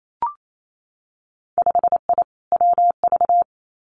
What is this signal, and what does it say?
Morse code, heard as a keyed tone, about 28 wpm, 700 Hz. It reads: 5I WV
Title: Winter Field Day Exchanges Encore 28wpm